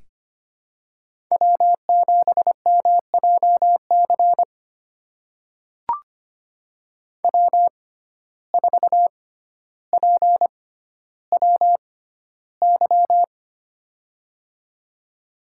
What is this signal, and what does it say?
Morse code, heard as a keyed tone, about 25 wpm, spaced as for 13 wpm, 700 Hz. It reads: W7MJC W4PWY
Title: USA Call Signs - 5 Characters - 6x Character Spacing - 25wpm